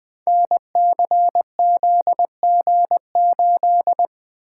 Morse code, heard as a keyed tone, about 20 wpm, 700 Hz. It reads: NCZG8
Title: Code Groups - 5 Letter:Numbers - 20wpm